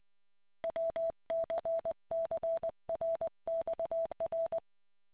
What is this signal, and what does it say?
Morse code, heard as a keyed tone, about 20 wpm, 660 Hz. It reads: WCCRXR